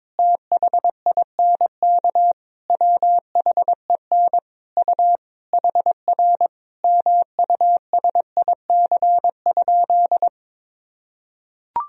Morse code, heard as a keyed tone, about 21 wpm, 700 Hz. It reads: THINK WHEN U HR MUSIC?
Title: Sentences from Top 300 Words and Abbreviations 22wpm